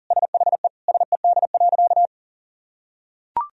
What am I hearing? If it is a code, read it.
Morse code, heard as a keyed tone, about 37 wpm, 700 Hz. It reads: SHE SED.
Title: Sentences from Top 100 Words and Abbreviations 40wpm